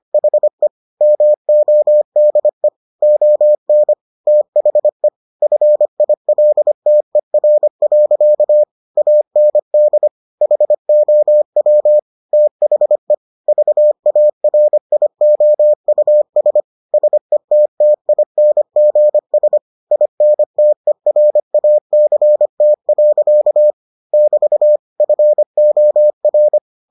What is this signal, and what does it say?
Morse code, heard as a keyed tone, about 25 wpm, 600 Hz. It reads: HE MODE ON THE FILTER. AND HOW THE VARIOUS SETTINGS INTERACT. <BT> FOR